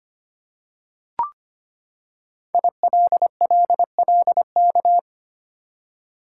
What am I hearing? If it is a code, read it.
Morse code, heard as a keyed tone, about 25 wpm, 700 Hz. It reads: ILLLK
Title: Code Groups - 5 Letters - 25wpm